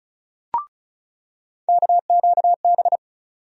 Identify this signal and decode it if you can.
Morse code, heard as a keyed tone, about 36 wpm, 700 Hz. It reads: KQB
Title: USA Call Sign Postfixes - 3 Letters 35wpm